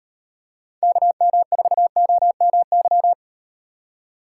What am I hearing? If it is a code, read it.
Morse code, heard as a keyed tone, about 38 wpm, 700 Hz. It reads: KM4OMY